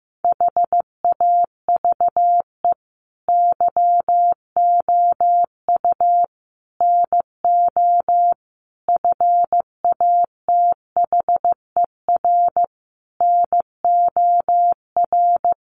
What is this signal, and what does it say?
Morse code, heard as a keyed tone, about 15 wpm, 700 Hz. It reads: HAVE YOU NO FATHER NOR
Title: Sentences from Top 1500 Words 15wpm